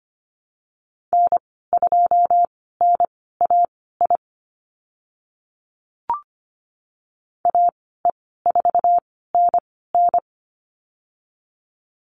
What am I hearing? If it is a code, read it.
Morse code, heard as a keyed tone, about 25 wpm, spaced as for 16 wpm, 690 Hz. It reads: N2NAI AE4NN
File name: Lesson 15g - Number 5 - Call Signs 25wpm